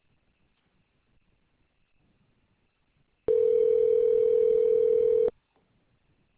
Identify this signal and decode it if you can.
Ringback tone (North America), 440 + 480 Hz.